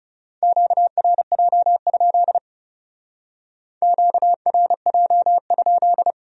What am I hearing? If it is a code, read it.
Morse code, heard as a keyed tone, about 36 wpm, 700 Hz. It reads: QRJ? QRJ?
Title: Speed-Racing Q-Codes - 30wpm